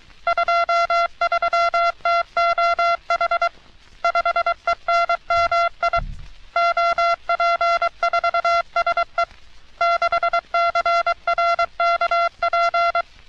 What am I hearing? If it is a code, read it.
Morse code, heard as a keyed tone, about 23 wpm, 680 Hz. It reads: 23TOH 5ENMI OP4SE 6CRKP
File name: U.S. Navy - RN-2849-124 - Sixteens Morse Code Practice